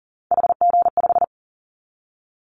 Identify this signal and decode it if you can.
Morse code, heard as a keyed tone, about 42 wpm, 700 Hz. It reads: HG5
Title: Callsign Prefixes for Top 50 DXCC - 40wpm